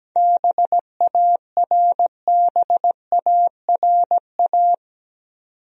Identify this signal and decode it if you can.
Morse code, heard as a keyed tone, about 17 wpm, 700 Hz. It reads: BARBARA
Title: USA Common Names - 17wpm